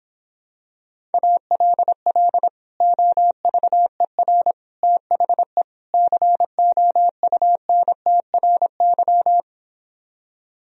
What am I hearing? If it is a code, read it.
Morse code, heard as a keyed tone, about 26 wpm, 700 Hz. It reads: ALL OVER THE COUNTRY